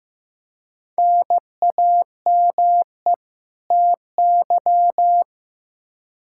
Morse code, heard as a keyed tone, about 15 wpm, 700 Hz. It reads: NAME TY
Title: Practice QSO - Exchange 1 - Beginning - 15wpm